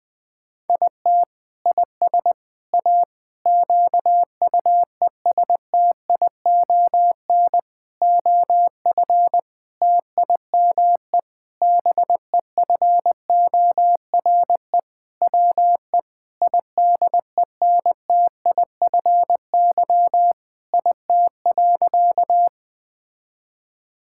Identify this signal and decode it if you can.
Morse code, heard as a keyed tone, about 20 wpm, 700 Hz. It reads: IT IS A QUESTION OF TIME BEFORE WE IDENTIFY IT.
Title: Sentences from Top 1000 Words - Review Part 6 - 20wpm